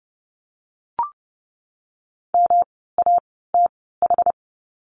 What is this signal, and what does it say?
Morse code, heard as a keyed tone, about 30 wpm, spaced as for 17 wpm, 700 Hz. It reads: MATH
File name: course / 30:10wpm / 108 Lesson 18f - Letter m - Words 30wpm